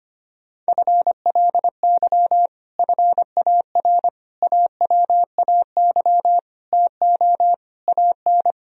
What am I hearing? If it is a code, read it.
Morse code, heard as a keyed tone, about 25 wpm, 700 Hz. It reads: FLY FAR AWAY TO AN
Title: Sentences from Top 1000 Words - Review Part 2 - 25wpm